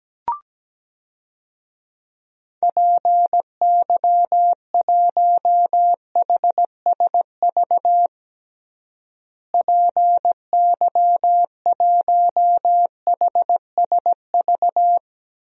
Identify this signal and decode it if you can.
Morse code, heard as a keyed tone, about 17 wpm, 700 Hz. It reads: PY1HSV PY1HSV
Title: DXCC-100 Call Signs - 6 Characters - Repeated 2x - 17wpm